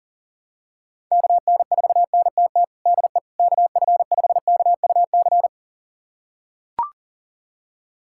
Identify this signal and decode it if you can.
Morse code, heard as a keyed tone, about 40 wpm, 700 Hz. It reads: KN4NTT DE KF5KUC
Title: USA Callsign DE Callsign 40wpm